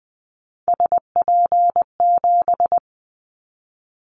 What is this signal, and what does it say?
Morse code, heard as a keyed tone, about 20 wpm, 700 Hz. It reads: SP7
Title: Callsign Prefixes for Top 25 DXCC - Repeated 3x - 20wpm